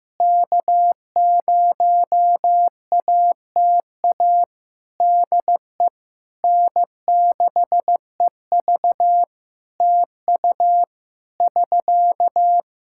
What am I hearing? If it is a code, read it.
Morse code, heard as a keyed tone, about 15 wpm, 700 Hz. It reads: K0ATA DE N6EV TU <SK>
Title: Practice QSO - Exchange 4 - Ending - 15wpm